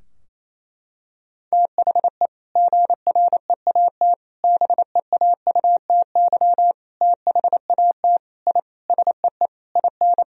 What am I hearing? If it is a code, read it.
Morse code, heard as a keyed tone, about 28 wpm, 700 Hz. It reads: THE GREAT BEAUTY THAT I SEE IN
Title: Sentences from Top 1000 Words - Review Part 2 - 28wpm